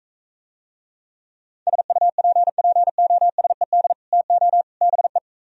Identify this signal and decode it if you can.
Morse code, heard as a keyed tone, about 44 wpm, 700 Hz. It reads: SUPPOSED TO BE